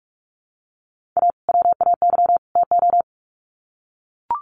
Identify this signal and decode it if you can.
Morse code, heard as a keyed tone, about 44 wpm, 700 Hz. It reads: A WAY TO
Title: Sets of 3 Words 45wpm